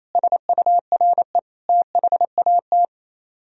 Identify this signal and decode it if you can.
Morse code, heard as a keyed tone, about 34 wpm, 700 Hz. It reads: SURE THAT
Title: Speed-Racing - Sets of 4 Words 28wpm